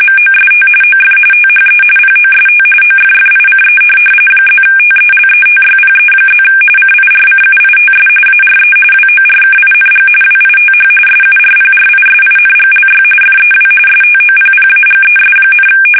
RTTY, about 45.45 baud. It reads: MOST OF THE POST OFFICE +TELETYPE+ CIRCUITS ARE WORKING WITH SURPRISINGLY FEW STOPPAGES,